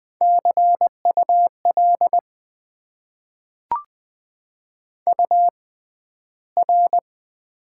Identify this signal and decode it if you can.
Morse code, heard as a keyed tone, about 20 wpm, spaced as for 12 wpm, 700 Hz. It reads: CUL UR
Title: Top 90 QSO Elements - 6x Character Spacing - 20wpm